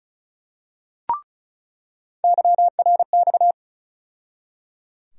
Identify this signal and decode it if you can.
Morse code, heard as a keyed tone, about 34 wpm, 700 Hz. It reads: YRX
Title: USA Call Sign Postfixes - 3 Letters 35wpm